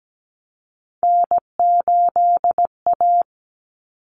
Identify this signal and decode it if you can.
Morse code, heard as a keyed tone, about 17 wpm, 700 Hz. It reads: N8A
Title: Common Cut Numbers - 3-Digit 17wpm